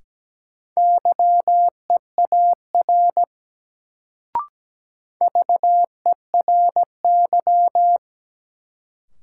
Morse code, heard as a keyed tone, about 17 wpm, 700 Hz. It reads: YEAR VERY
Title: Sets of 2 Words - New Word Drill - 17wpm